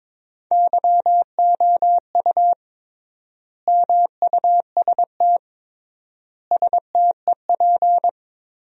Morse code, heard as a keyed tone, about 22 wpm, 700 Hz. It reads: YOU MUST STEP
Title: 3x Word-Spacing - Sentences from Top 400 Words - 22wpm